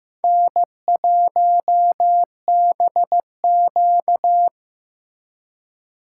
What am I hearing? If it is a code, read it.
Morse code, heard as a keyed tone, about 18 wpm, 700 Hz. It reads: N1BQ
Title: Speed Racing - USA 1x2 Call Signs 15wpm